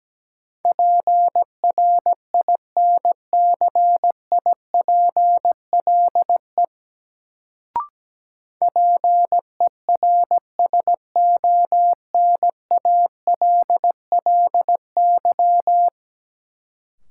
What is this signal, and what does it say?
Morse code, heard as a keyed tone, about 17 wpm, 700 Hz. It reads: PRINCIPLE PERSONALLY
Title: Sentences from Top 1400 Words - New Word Drill - 17wpm